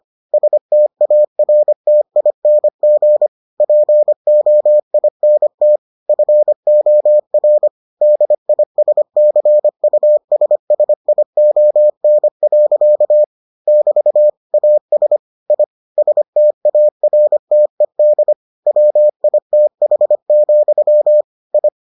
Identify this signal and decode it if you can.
Morse code, heard as a keyed tone, about 25 wpm, 600 Hz. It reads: STARTING POINT FOR DISCUSSION. <BT> AS I STARTED WITH, I